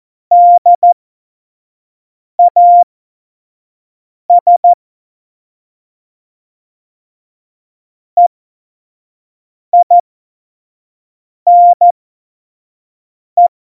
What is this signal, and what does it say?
Morse code, heard as a keyed tone, about 14 wpm, spaced as for 5 wpm, 700 Hz. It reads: DAS EINE
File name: Morsekurs / Witze / Witz 19. Februar 2019, 25 F70